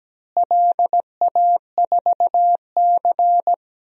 Morse code, heard as a keyed tone, about 17 wpm, 700 Hz. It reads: LA4C
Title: DXCC-100 Call Signs - 4 Characters - Repeated 2x - 17wpm